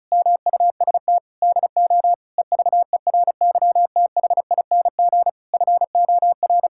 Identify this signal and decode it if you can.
Morse code, heard as a keyed tone, about 36 wpm, 700 Hz. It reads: MUST DO EVERYTHING FOR
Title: Sentences from Top 1000 Words - Review Part 6 - 35wpm